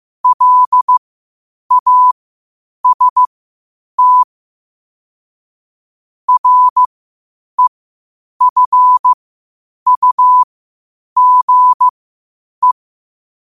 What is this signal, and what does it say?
Morse code, heard as a keyed tone, about 15 wpm, spaced as for 8 wpm, 1000 Hz. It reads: LAST REFUGE
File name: 8wpm